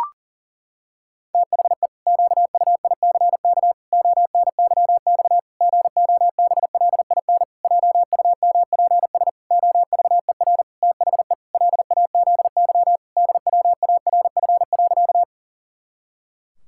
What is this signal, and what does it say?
Morse code, heard as a keyed tone, about 40 wpm, 700 Hz. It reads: THE QUICK ONYX GOBLIN JUMPS OVER THE LAZY DWARF.
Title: Pangram Challenge v2 - 40wpm